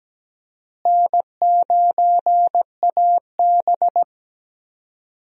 Morse code, heard as a keyed tone, about 17 wpm, 700 Hz. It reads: N9AB
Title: USA Call Signs - 4 Characters - Repeated 2x - 17wpm